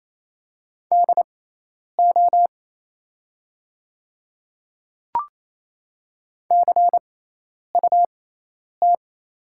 Morse code, heard as a keyed tone, about 28 wpm, spaced as for 10 wpm, 690 Hz. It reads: DO CUT
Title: 6x Character Spacing - Top 300 Words - 28wpm